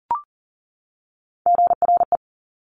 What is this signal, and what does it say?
Morse code, heard as a keyed tone, about 40 wpm, 700 Hz. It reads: GRE
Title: Top Three Letter Combinations in Words 40wpm